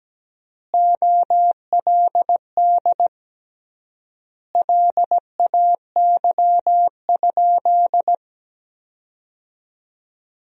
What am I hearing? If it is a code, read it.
Morse code, heard as a keyed tone, about 17 wpm, 700 Hz. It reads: OLD LAY?
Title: 3x Word-Spacing - Sentences from Top 400 Words - 17wpm